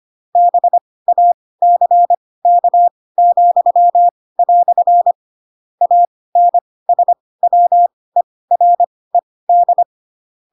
Morse code, heard as a keyed tone, about 25 wpm, spaced as for 18 wpm, 700 Hz. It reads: BACK," ANSWERED
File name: Pinocchio / Pinocchio ch6 12 25 ik1hge